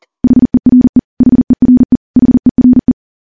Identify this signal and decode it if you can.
Morse code, heard as a keyed tone, about 39 wpm, 250 Hz. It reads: HERE HERE HERE
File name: morse / morse-training-40WPM